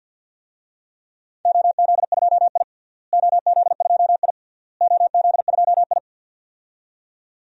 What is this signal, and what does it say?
Morse code, heard as a keyed tone, about 48 wpm, 700 Hz. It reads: OZ2I OZ2I OZ2I
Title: DXCC-100 Call Signs - 4 Characters - Repeated 3x - 50wpm